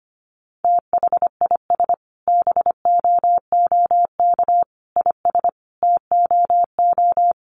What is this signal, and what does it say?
Morse code, heard as a keyed tone, about 25 wpm, 700 Hz. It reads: THIS BOOK IS TOO
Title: Additional Easy Phrases 25wpm